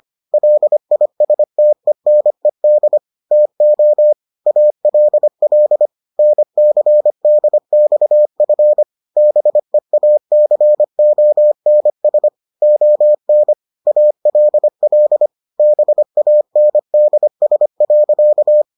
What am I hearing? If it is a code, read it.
Morse code, heard as a keyed tone, about 25 wpm, 600 Hz. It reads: LISTENED TO ALL NCDXF BEACONS ON ALL BANDS.